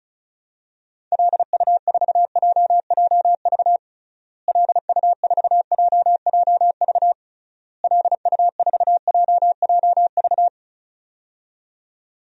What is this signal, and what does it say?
Morse code, heard as a keyed tone, about 34 wpm, 700 Hz. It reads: LU4JJV LU4JJV LU4JJV
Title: DXCC-100 Call Signs - 6 Characters - Repeated 3x - 35wpm